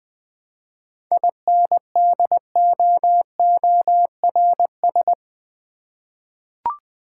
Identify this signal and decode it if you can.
Morse code, heard as a keyed tone, about 20 wpm, 700 Hz. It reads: INDOORS
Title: 7 Letter Words - First 3 Letters Given - 20wpm